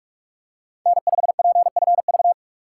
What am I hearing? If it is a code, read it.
Morse code, heard as a keyed tone, about 43 wpm, 700 Hz. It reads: N5PFV
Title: USA Call Signs - 2x Character Spacing - 45wpm